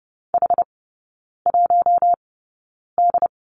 Morse code, heard as a keyed tone, about 30 wpm, spaced as for 9 wpm, 700 Hz. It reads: H1D
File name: Lesson 16e - Letter c - Three Characters 30wpm